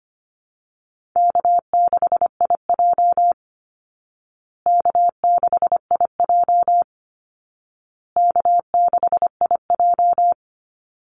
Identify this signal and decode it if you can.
Morse code, heard as a keyed tone, about 25 wpm, 700 Hz. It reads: K6IJ K6IJ K6IJ
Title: Easy Call Signs Repeated 3x - 25wpm